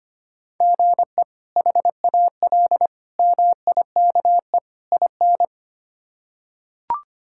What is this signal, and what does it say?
Morse code, heard as a keyed tone, about 25 wpm, 700 Hz. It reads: GE HAL MIKE IN
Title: K1USN SST QSO Practice - 25wpm